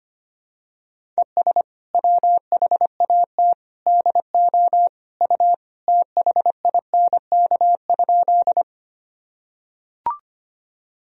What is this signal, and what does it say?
Morse code, heard as a keyed tone, about 25 wpm, 700 Hz. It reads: ES WHAT DO U THINK?
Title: Sentences from Top 200 Words and Abbreviations 25wpm